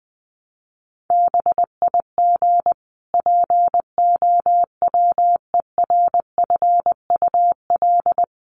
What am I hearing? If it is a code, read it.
Morse code, heard as a keyed tone, about 20 wpm, 700 Hz. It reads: BIG POWERFUL
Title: Sentences from Top 1300 Words 20wpm